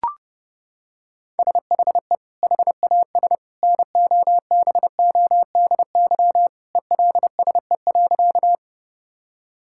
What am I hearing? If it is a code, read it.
Morse code, heard as a keyed tone, about 30 wpm, 700 Hz. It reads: SHE HAS NOBODY ELSE.
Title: Sentences from Top 1600 Words 30wpm